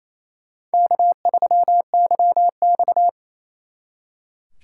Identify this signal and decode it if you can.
Morse code, heard as a keyed tone, about 33 wpm, 700 Hz. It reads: K3YX
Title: Speed Racing - USA 1x2 Call Signs 28wpm